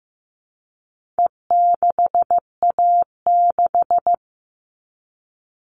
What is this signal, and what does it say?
Morse code, heard as a keyed tone, about 15 wpm, 700 Hz. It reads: E6A6
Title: All Cut Numbers - 4-Digit 15wpm